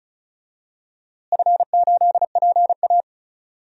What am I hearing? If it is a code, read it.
Morse code, heard as a keyed tone, about 34 wpm, 700 Hz. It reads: F8PA
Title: DXCC-100 Call Signs - 4 Characters - Repeated 2x - 35wpm